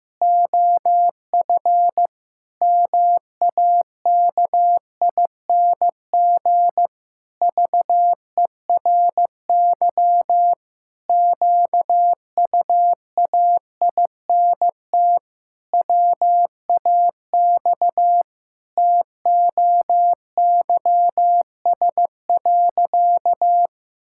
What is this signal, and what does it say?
Morse code, heard as a keyed tone, about 15 wpm, 700 Hz. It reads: OF MAKING VERY QUAINT WAX TOYS.
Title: Pangram Challenge v2 - 15wpm